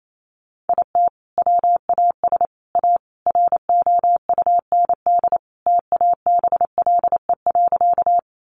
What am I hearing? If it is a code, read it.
Morse code, heard as a keyed tone, about 28 wpm, 700 Hz. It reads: IT WAS A ROUND TABLE.
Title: Sentences from Top 400 Words 28wpm